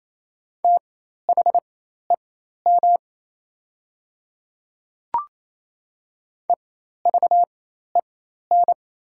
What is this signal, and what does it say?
Morse code, heard as a keyed tone, about 27 wpm, spaced as for 13 wpm, 690 Hz. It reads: THEM EVEN
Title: 4x Character Spacing - Top 200 Words - 28wpm